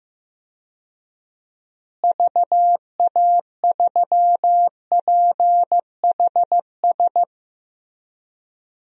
Morse code, heard as a keyed tone, about 15 wpm, 700 Hz. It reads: VA3PHS
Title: DXCC-100 Call Signs - 6 Characters - Repeated 3x - 15wpm